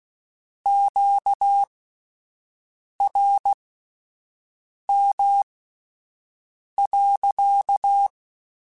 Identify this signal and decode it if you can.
Morse code, heard as a keyed tone, about 16 wpm, spaced as for 5 wpm, 800 Hz. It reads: QRM.
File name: Q Calls / Q Calls-mp3s / Q Calls sppc @10